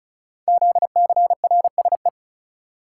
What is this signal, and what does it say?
Morse code, heard as a keyed tone, about 36 wpm, 700 Hz. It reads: ZCRSE
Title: Code Groups - 5 Letters - 35wpm